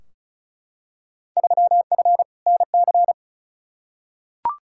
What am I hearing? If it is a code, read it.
Morse code, heard as a keyed tone, about 36 wpm, 690 Hz. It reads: 3F NC
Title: Field Day Exchanges Encore - 35wpm